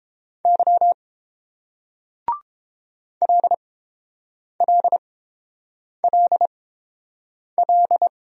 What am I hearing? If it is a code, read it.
Morse code, heard as a keyed tone, about 30 wpm, spaced as for 7 wpm, 700 Hz. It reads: Y LLLL